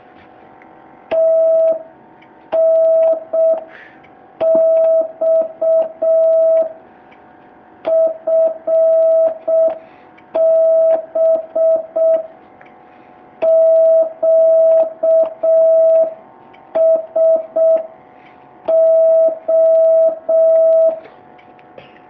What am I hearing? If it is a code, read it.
Morse code, heard as a keyed tone, about 6 wpm, 650 Hz. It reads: TNX FB QSO
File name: jh8cbh / 2019cw